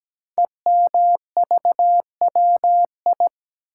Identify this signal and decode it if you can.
Morse code, heard as a keyed tone, about 17 wpm, 700 Hz. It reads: EMVWI
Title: Code Groups - 5 Letters - 17wpm